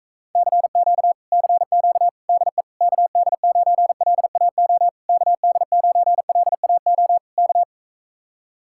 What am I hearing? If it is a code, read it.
Morse code, heard as a keyed tone, about 45 wpm, 700 Hz. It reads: CQ CQ DE KD9LAO KD9LAO K